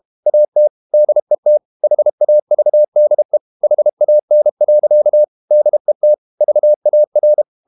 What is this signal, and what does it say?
Morse code, heard as a keyed tone, about 32 wpm, 600 Hz. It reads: AT DET HAVDE HAN. DET VAR
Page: Ferie 32 wpm